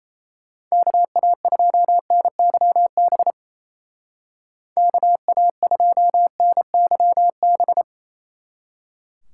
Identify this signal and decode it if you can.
Morse code, heard as a keyed tone, about 37 wpm, 700 Hz. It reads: KA2NYB KA2NYB